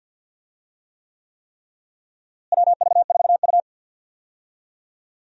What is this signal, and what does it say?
Morse code, heard as a keyed tone, about 52 wpm, 700 Hz. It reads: WV4U
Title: USA Call Signs - 4 Characters - Repeated 3x - 50wpm